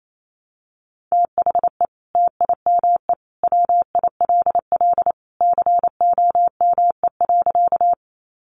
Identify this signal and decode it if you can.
Morse code, heard as a keyed tone, about 28 wpm, 700 Hz. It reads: THE TIME WILL COME.
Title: Sentences from Top 100 Words 28wpm